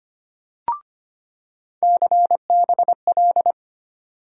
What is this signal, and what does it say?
Morse code, heard as a keyed tone, about 25 wpm, 700 Hz. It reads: CBL
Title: USA Call Sign Postfixes - 3 Letters 25wpm